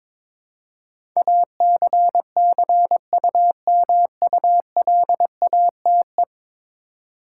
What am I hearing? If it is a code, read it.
Morse code, heard as a keyed tone, about 22 wpm, 700 Hz. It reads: ACCUMULATE
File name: practice / 22wpm / 10 Letter Words - First 3 Letters Given - 22wpm